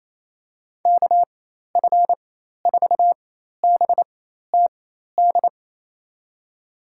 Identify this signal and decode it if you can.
Morse code, heard as a keyed tone, about 28 wpm, spaced as for 13 wpm, 700 Hz. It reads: KF4BTD
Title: USA Call Signs - 6 Characters - 4x Character Spacing - 28wpm